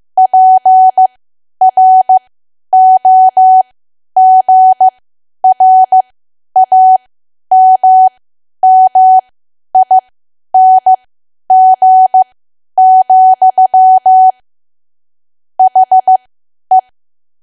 Morse code, heard as a keyed tone, about 15 wpm, spaced as for 10 wpm, 750 Hz. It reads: PROGRAMMING, HE
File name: cwa / W1AW-Level2 / 10WPM